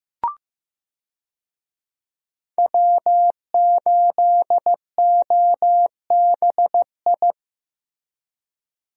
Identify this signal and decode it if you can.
Morse code, heard as a keyed tone, about 15 wpm, 700 Hz. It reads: W8OBI
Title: USA Call Signs - 5 Characters - Repeated 2x - 15wpm